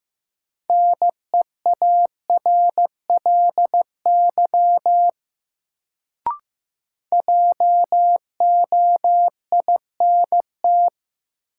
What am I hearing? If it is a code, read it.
Morse code, heard as a keyed tone, about 15 wpm, 700 Hz. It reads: NEARLY JOINT
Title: Sentences from Top 1300 Words - New Word Drill - 15wpm